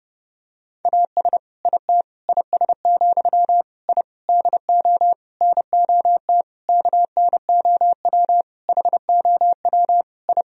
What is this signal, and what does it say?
Morse code, heard as a keyed tone, about 30 wpm, 700 Hz. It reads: AS IT IS, I DO NOT KNOW HOW I
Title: Sentences from Top 1000 Words - Review Part 2 - 30wpm